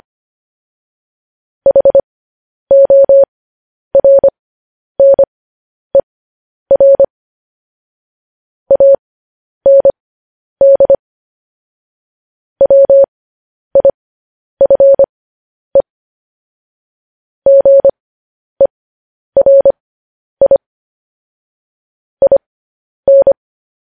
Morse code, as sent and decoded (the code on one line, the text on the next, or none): .... --- .-. -. .-. / .- -. -.. / .-- .. ..-. / --. .-. .. / .. -.
HORNER AND WIFE GERI IN